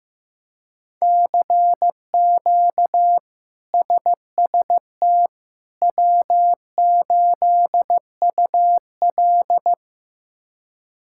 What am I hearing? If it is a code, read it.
Morse code, heard as a keyed tone, about 15 wpm, 700 Hz. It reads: CQ SST W8UL
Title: K1USN SST QSO Practice - 15wpm